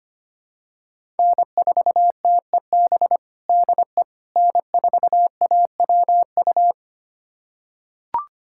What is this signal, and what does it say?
Morse code, heard as a keyed tone, about 25 wpm, 700 Hz. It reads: N4TEB DE N4AWU
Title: USA Callsign DE Callsign - 5 Characters - 25wpm